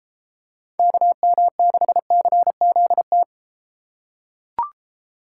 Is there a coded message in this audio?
Yes: Morse code from 0.8 to 3.2 s, heard as a keyed tone, about 32 wpm, 700 Hz. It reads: KM6CZT